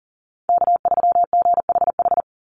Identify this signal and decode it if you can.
Morse code, heard as a keyed tone, about 40 wpm, 700 Hz. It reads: K3GHH
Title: Call Signs - 40wpm